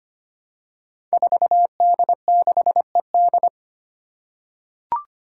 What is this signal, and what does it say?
Morse code, heard as a keyed tone, about 25 wpm, 700 Hz. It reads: 4D6ED
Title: Five Challenging Letter-Number Combinations 25wpm